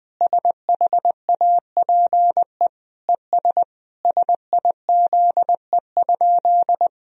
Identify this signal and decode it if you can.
Morse code, heard as a keyed tone, about 20 wpm, 700 Hz. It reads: SHAPE ES SIZE?
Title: Sentences from Top 500 Words and Abbreviations 20wpm